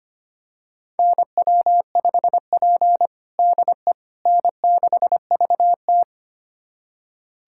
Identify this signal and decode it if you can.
Morse code, heard as a keyed tone, about 25 wpm, 700 Hz. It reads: NW5P DE N6VT
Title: USA Callsign DE Callsign - 4 Characters - 25wpm